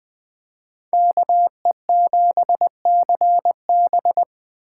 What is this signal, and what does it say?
Morse code, heard as a keyed tone, about 21 wpm, 700 Hz. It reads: KE7CB
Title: Speed Racing - USA Call Signs 17wpm